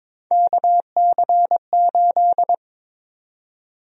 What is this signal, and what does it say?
Morse code, heard as a keyed tone, about 22 wpm, 700 Hz. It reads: KC8
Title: USA Call Sign Prefixes - 22wpm